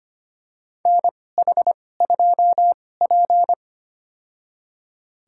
Morse code, heard as a keyed tone, about 25 wpm, spaced as for 18 wpm, 700 Hz. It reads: NH2P
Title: USA Call Signs - 4 Characters - 2x Character Spacing - 25wpm